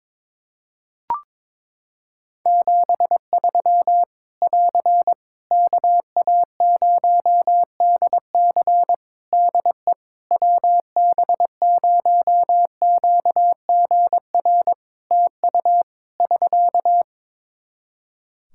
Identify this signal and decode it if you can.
Morse code, heard as a keyed tone, about 22 wpm, 700 Hz. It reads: 73 <AR> KA0DC DE WB0QGR TU <SK>
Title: Practice QSO - Exchange 4 - Ending - 22wpm